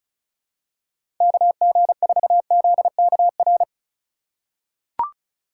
Morse code, heard as a keyed tone, about 34 wpm, 700 Hz. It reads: KG4ZKR
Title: USA Call Signs - 6 Characters - 4x Character Spacing - 35wpm